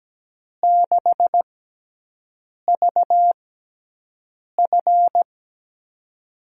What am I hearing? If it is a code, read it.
Morse code, heard as a keyed tone, about 17 wpm, spaced as for 6 wpm, 700 Hz. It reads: 6VF
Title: USA Call Signs - 5 Characters - 6x Character Spacing - 17wpm